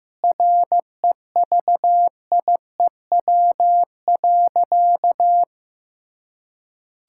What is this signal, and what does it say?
Morse code, heard as a keyed tone, about 15 wpm, 700 Hz. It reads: REVIEW.
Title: Sentences from Top 1000 Words - Review Part 6 - 15wpm